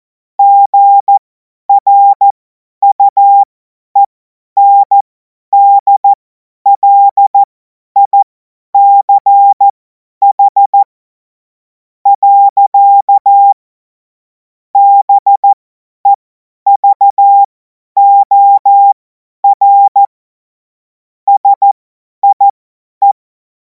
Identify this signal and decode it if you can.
Morse code, heard as a keyed tone, about 14 wpm, spaced as for 10 wpm, 800 Hz. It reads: GRUENDLICH. BEVOR SIE
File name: Morsekurs / Witze / Witz 28. April 2026 50 F70